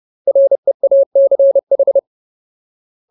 Morse code, heard as a keyed tone, about 30 wpm, 550 Hz. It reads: REACH
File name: Ultimate Word List / 30WPM Ultimate